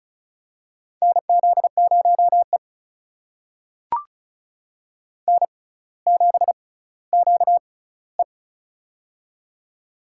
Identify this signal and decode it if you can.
Morse code, heard as a keyed tone, about 36 wpm, spaced as for 18 wpm, 700 Hz. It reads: NZ0E N7QE
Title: USA Call Signs - 4 Characters - 6x Character Spacing - 35wpm